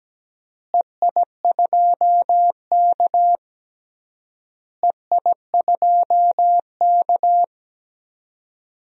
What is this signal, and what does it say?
Morse code, heard as a keyed tone, about 17 wpm, 700 Hz. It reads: EI2K EI2K
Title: DXCC-100 Call Signs - 4 Characters - Repeated 3x - 17wpm